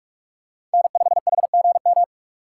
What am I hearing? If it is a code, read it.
Morse code, heard as a keyed tone, about 44 wpm, 690 Hz. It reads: N5HGM
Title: USA Call Signs - 5 Characters - 6x Character Spacing - 45wpm